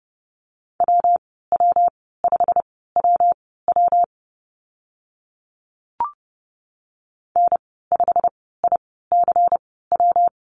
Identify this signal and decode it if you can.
Morse code, heard as a keyed tone, about 30 wpm, spaced as for 17 wpm, 700 Hz. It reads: WW5WW N5ICW
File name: Lesson 19g - Letter w - Call Signs 30wpm